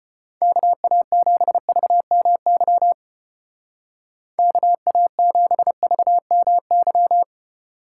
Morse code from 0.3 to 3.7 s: KA7VMY

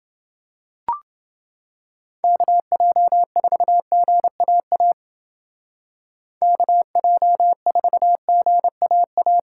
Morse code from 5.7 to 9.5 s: KJ4GAA

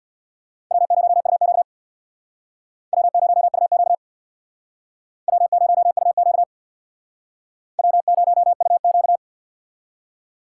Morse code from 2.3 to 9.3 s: W0UQ W0UQ W0UQ

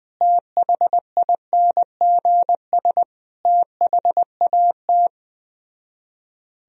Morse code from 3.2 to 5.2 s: THAT

